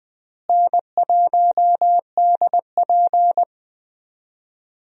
Morse code, heard as a keyed tone, about 20 wpm, 700 Hz. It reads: N1DP
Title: USA Call Signs - 4 Characters - Repeated 2x - 20wpm